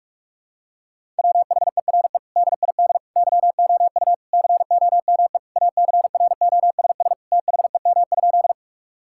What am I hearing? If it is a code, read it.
Morse code, heard as a keyed tone, about 46 wpm, 700 Hz. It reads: WHERE DID YOU COME ACROSS THEM?
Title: Sentences from Top 1000 Words - Review Part 5 - 45wpm